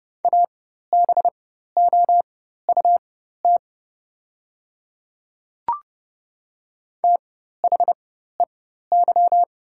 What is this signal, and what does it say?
Morse code, heard as a keyed tone, about 30 wpm, spaced as for 14 wpm, 700 Hz. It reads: ABOUT THEY
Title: 4x Character Spacing - Top 200 Words - 30wpm